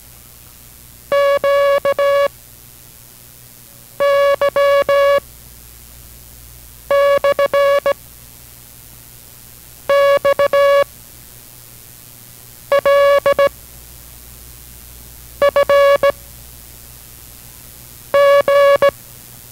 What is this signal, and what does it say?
Morse code, heard as a keyed tone, about 17 wpm, 560 Hz. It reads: Q Y / X L F G